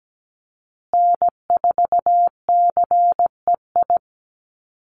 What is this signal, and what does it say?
Morse code, heard as a keyed tone, about 17 wpm, 700 Hz. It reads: N4CEI
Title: USA Call Signs Repeated 2x 17wpm